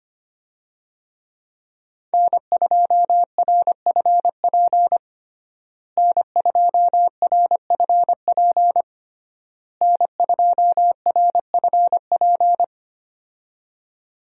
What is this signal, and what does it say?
Morse code, heard as a keyed tone, about 25 wpm, 700 Hz. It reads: N2RFP N2RFP N2RFP
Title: USA Call Signs - 5 Characters - Repeated 3x - 25wpm